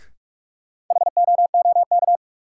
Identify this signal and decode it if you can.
Morse code, heard as a keyed tone, about 46 wpm, 700 Hz. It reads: HOOK
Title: Sentences from Top 1600 Words - New Word Drill Alt - 45wpm